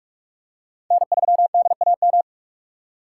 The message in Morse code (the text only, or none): N3DAM